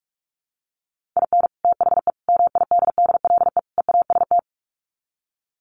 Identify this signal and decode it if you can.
Morse code, heard as a keyed tone, about 44 wpm, 700 Hz. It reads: IN THE MIDDLE EAST
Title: Sets of 4 Words 45wpm